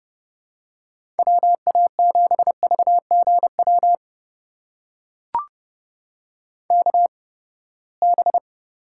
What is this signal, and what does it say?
Morse code, heard as a keyed tone, about 30 wpm, spaced as for 21 wpm, 700 Hz. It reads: WA7VGW KB
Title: USA Call Signs - 8x Character Spacing - 30wpm